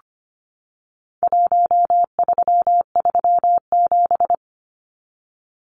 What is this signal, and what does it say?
Morse code, heard as a keyed tone, about 25 wpm, 700 Hz. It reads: 1337
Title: Numbers - 4-Digit 25wpm